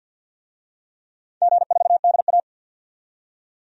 Morse code, heard as a keyed tone, about 48 wpm, 700 Hz. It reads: G4DA
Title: DXCC-100 Call Signs - 4 Characters - 50wpm